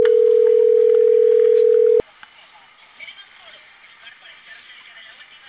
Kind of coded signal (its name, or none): telephone call progress tone